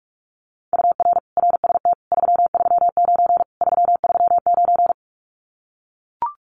Morse code, heard as a keyed tone, about 44 wpm, 700 Hz. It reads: UR RST 339 339
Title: Common QSO Phrases Encore 45wpm